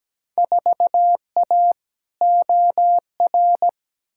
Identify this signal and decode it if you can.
Morse code, heard as a keyed tone, about 17 wpm, 700 Hz. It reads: 4A OR
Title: Field Day Exchanges - 17wpm